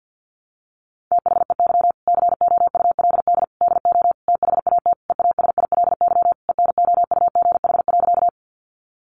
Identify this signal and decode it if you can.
Morse code, heard as a keyed tone, about 50 wpm, 700 Hz. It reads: THEY COULD DO THAT EASILY ENOUGH.
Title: Sentences from Top 900 Words 50wpm